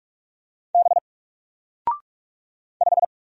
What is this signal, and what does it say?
Morse code, heard as a keyed tone, about 45 wpm, 700 Hz. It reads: B 5
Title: Single Letter-Number 45wpm